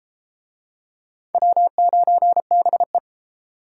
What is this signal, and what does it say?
Morse code, heard as a keyed tone, about 30 wpm, 700 Hz. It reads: W9BE